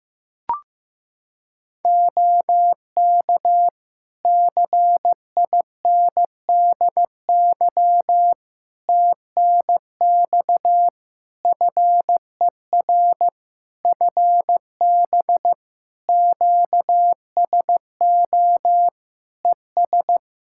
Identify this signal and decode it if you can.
Morse code, heard as a keyed tone, about 15 wpm, 700 Hz. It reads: OK CINDY TNX FER FB QSO ES
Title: K1USN SST QSO Practice 15wpm